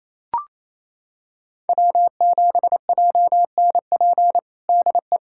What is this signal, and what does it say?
Morse code, heard as a keyed tone, about 28 wpm, 700 Hz. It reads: W7JNP DE
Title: USA Callsign DE Callsign 28wpm